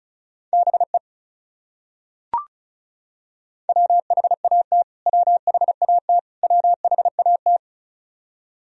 Morse code, heard as a keyed tone, about 34 wpm, 700 Hz. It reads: BE WHAT WHAT WHAT
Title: Top 100 Words Repeated 35wpm